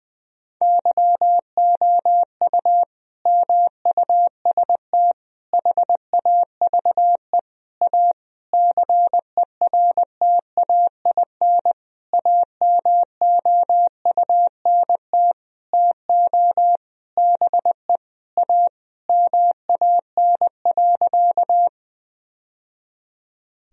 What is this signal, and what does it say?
Morse code, heard as a keyed tone, about 20 wpm, 700 Hz. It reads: YOU MUST HAVE A CERTAIN AMOUNT TO BE A MAN.
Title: Sentences from Top 1000 Words - Review Part 4 - 20wpm